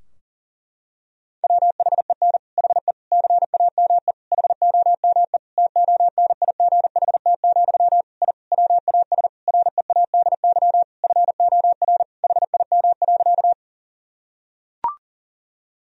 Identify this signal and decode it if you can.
Morse code, heard as a keyed tone, about 40 wpm, 700 Hz. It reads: WHEN HE CAME HOME TONIGHT, I WAS READY FOR HIM.
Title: Sentences from Top 1000 Words - Review Part 3 - 40wpm